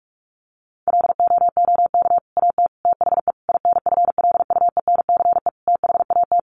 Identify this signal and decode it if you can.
Morse code, heard as a keyed tone, about 44 wpm, 700 Hz. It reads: LOOK AT THE INFLUENCE THAT
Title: Sentences from Top 1100 Words 45wpm